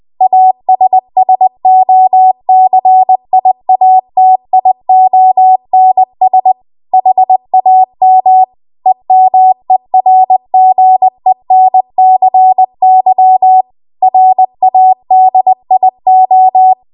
Morse code, heard as a keyed tone, about 20 wpm, 750 Hz. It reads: ASSOCIATIONS HAM EMERGENCY RADIO
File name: Archive / 20 WPM / 20WPM